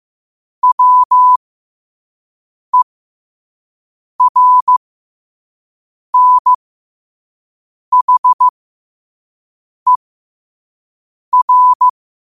Morse code, heard as a keyed tone, about 15 wpm, spaced as for 5 wpm, 1000 Hz. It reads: WERNHER
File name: qotdfiles / 5wpm